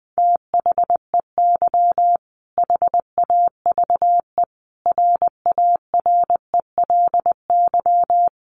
Morse code, heard as a keyed tone, about 20 wpm, 700 Hz. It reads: THEY HAVE RARELY
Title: Sentences from Top 1400 Words 20wpm